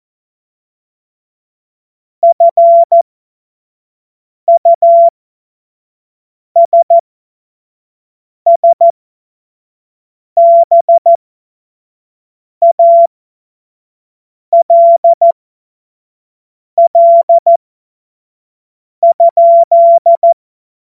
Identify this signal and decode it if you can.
Morse code, heard as a keyed tone, about 14 wpm, spaced as for 5 wpm, 680 Hz. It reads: FUSSBALL?
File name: Morsekurs / Witze / Witz 09. Februar 2021, 25 F70